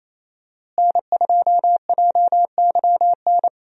Morse code, heard as a keyed tone, about 28 wpm, 700 Hz. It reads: N2JYN
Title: USA Call Signs - 5 Characters - Repeated 3x - 28wpm